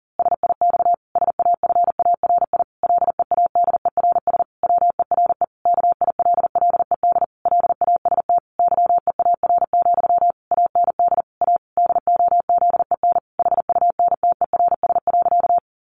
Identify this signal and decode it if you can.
Morse code, heard as a keyed tone, about 40 wpm, 700 Hz. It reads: SIX SAFARI LEADERS WERE KILLED LAST YEAR, AND A DOZEN HUNTERS.
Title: A Sound Of Thunder 40wpm